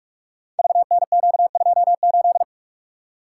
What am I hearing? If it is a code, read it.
Morse code, heard as a keyed tone, about 44 wpm, 690 Hz. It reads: VNQ28
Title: Code Groups - 5 Letter:Numbers - 45wpm